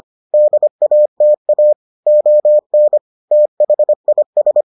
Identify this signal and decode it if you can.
Morse code, heard as a keyed tone, about 25 wpm, 600 Hz. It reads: DATA ON THIS